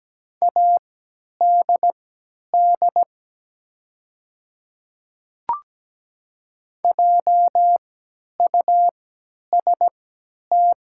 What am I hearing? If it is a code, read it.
Morse code, heard as a keyed tone, about 17 wpm, spaced as for 10 wpm, 690 Hz. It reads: ADD JUST
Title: 3x Character Spacing - Top 500 Words - 17wpm